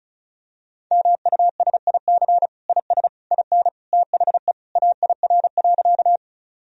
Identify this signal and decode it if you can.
Morse code, heard as a keyed tone, about 36 wpm, 700 Hz. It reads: MUSIC IS IN THE AIR.
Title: Sentences from Top 1000 Words - Review Part 1 - 35wpm